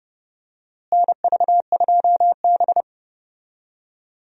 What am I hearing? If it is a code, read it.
Morse code, heard as a keyed tone, about 32 wpm, 700 Hz. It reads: NV2B
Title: USA Call Signs - 4 Characters - Repeated 2x - 30wpm